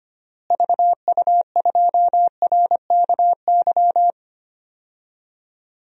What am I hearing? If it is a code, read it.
Morse code, heard as a keyed tone, about 25 wpm, 700 Hz. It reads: VU2RKY
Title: DXCC-100 Call Signs - 6 Characters - Repeated 3x - 25wpm